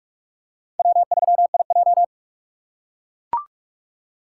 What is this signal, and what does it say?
Morse code, heard as a keyed tone, about 44 wpm, 700 Hz. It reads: W3IJ